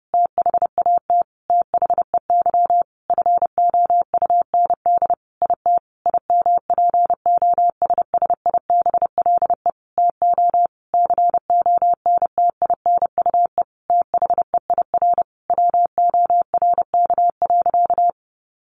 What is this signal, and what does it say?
Morse code, heard as a keyed tone, about 30 wpm, 700 Hz. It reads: THAT THEY FOUND IT IMPOSSIBLE TO CONTINUE THEIR WORK.
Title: Sentences from Top 1100 Words 30wpm